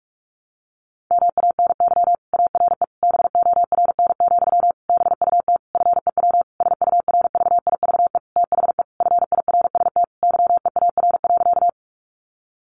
Morse code, heard as a keyed tone, about 46 wpm, 700 Hz. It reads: MANY ARE BORN, BUT FEW SURVIVE THE FIRST YEAR.
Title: Sentences from Top 1300 Words 45wpm